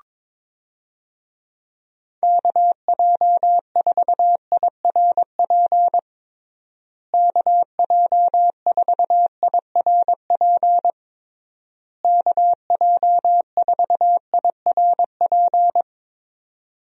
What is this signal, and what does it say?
Morse code, heard as a keyed tone, about 22 wpm, 700 Hz. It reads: KJ4IRP KJ4IRP KJ4IRP
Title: USA Call Signs - 6 Characters - Repeated 3x - 22wpm